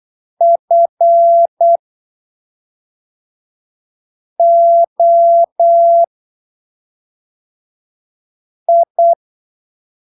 Morse code, heard as a keyed tone, about 8 wpm, spaced as for 3 wpm, 680 Hz. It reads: FOI
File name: arquivos / texto / texto 14 08ppm